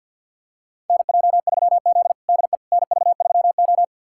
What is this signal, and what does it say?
Morse code, heard as a keyed tone, about 47 wpm, 700 Hz. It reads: NJ3Z DE NV3O
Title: USA Callsign DE Callsign - 4 Characters - 50wpm